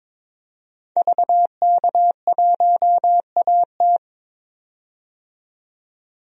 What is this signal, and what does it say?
Morse code, heard as a keyed tone, about 22 wpm, 700 Hz. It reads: VK1AT
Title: DXCC-100 Call Signs - 5 Characters - 22wpm